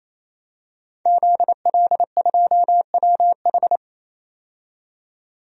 Morse code, heard as a keyed tone, about 28 wpm, 700 Hz. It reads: ZL2WH
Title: DXCC-100 Call Signs - 5 Characters - 28wpm